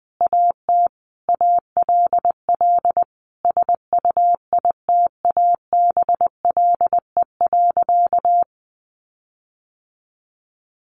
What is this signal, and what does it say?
Morse code, heard as a keyed tone, about 20 wpm, 700 Hz. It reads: AT ALL SUITABLE.
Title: Sentences from Top 1400 Words 20wpm